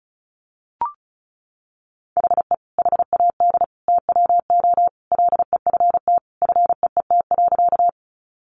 Morse code, heard as a keyed tone, about 34 wpm, 700 Hz. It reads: HE HAD TWO LEFT FEET.
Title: Sentences from Top 300 Words 35wpm